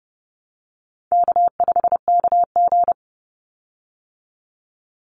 Morse code, heard as a keyed tone, about 30 wpm, 700 Hz. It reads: K5KG
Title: Easy Call Signs Repeated 3x - 30wpm